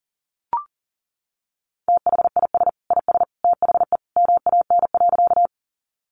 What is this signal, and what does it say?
Morse code, heard as a keyed tone, about 40 wpm, 700 Hz. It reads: THIS IS THE MAN.
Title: Sentences from Top 200 Words 40wpm